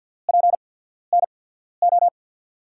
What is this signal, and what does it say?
Morse code, heard as a keyed tone, about 47 wpm, spaced as for 14 wpm, 700 Hz. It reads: PNO
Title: USA Call Signs - 6 Characters - 8x Character Spacing - 50wpm